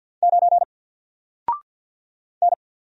Morse code, heard as a keyed tone, about 52 wpm, 700 Hz. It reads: 9 N
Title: Single Letter-Number Alt 50wpm